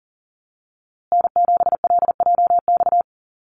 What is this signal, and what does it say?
Morse code, heard as a keyed tone, about 40 wpm, 700 Hz. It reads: N7LJX
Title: USA Call Signs Repeated 2x 40wpm